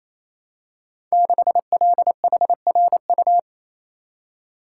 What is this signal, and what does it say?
Morse code, heard as a keyed tone, about 27 wpm, 700 Hz. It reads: 6LHRU
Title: Code Groups - 5 Letter:Numbers - 28wpm